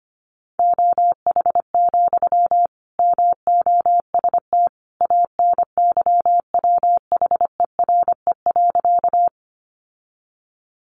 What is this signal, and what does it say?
Morse code, heard as a keyed tone, about 25 wpm, 700 Hz. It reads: OH, MOST ANYWHERE.
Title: Sentences from Top 1400 Words 25wpm